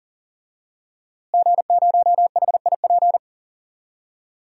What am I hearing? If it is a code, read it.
Morse code, heard as a keyed tone, about 40 wpm, 700 Hz. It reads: G0HIP
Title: DXCC-100 Call Signs - 5 Characters - Repeated 3x - 40wpm